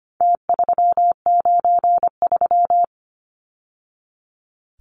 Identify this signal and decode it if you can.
Morse code, heard as a keyed tone, about 25 wpm, 700 Hz. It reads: T393
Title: Numbers - Contest Style - 4-Digit 25wpm